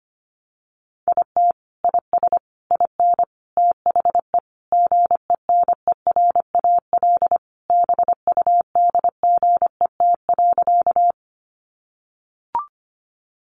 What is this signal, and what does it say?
Morse code, heard as a keyed tone, about 24 wpm, 700 Hz. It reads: IT IS IN THE GENERAL BUDGET.
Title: Sentences from Top 1200 Words 25wpm